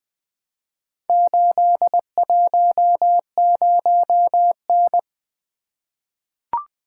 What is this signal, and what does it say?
Morse code, heard as a keyed tone, about 20 wpm, 700 Hz. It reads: 810N
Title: Four Letter-Number Combinations - 20wpm